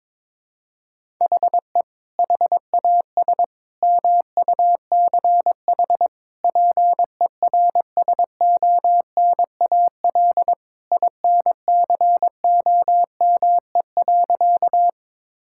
Morse code, heard as a keyed tone, about 22 wpm, 700 Hz. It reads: HE HAS MUCH PERSONAL INCOME.
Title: Sentences from Top 1000 Words - Review Part 5 - 22wpm